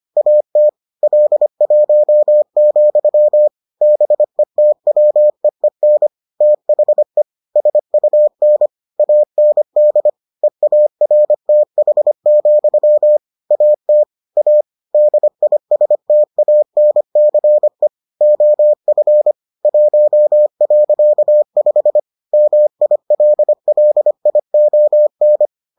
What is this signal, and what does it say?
Morse code, heard as a keyed tone, about 25 wpm, 600 Hz. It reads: AT L1, BETWEEN THE SUN AND EARTH, AT A DISTANCE OF 1.5 MILLION